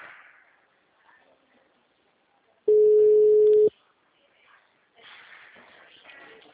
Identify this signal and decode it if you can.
Ringback tone (Europe, 425 Hz).
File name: out- -2003- - - .43528